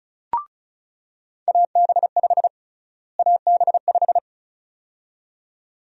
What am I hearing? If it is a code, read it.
Morse code, heard as a keyed tone, about 36 wpm, 700 Hz. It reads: AB5 AB5
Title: USA Call Sign Prefixes - Repeated 2x - 35wpm